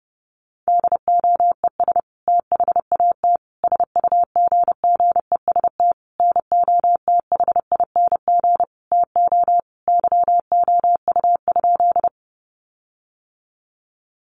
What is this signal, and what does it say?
Morse code, heard as a keyed tone, about 30 wpm, 700 Hz. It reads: DOES THAT SUGGEST NOTHING TO YOU?
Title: Sentences from Top 1300 Words 30wpm